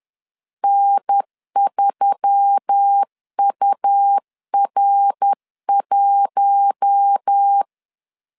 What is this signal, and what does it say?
Morse code, heard as a keyed tone, about 10 wpm, 800 Hz. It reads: N3UR1
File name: CW / DET / 1 6 1500C S B 10wpm 4137